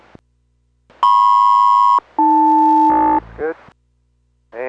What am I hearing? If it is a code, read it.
Aircraft SELCAL, heard as two pairs of tones, with speech. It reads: MP-AL